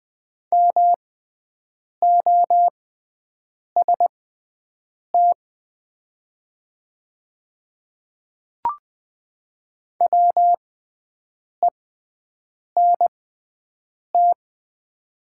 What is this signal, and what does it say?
Morse code, heard as a keyed tone, about 20 wpm, spaced as for 7 wpm, 700 Hz. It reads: MOST WENT
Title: 6x Character Spacing - Top 200 Words - 20wpm